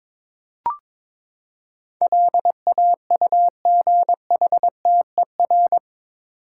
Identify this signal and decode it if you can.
Morse code, heard as a keyed tone, about 22 wpm, 700 Hz. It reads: LAUGHTER
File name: practice / 22wpm / Eight Letter Words - 22wpm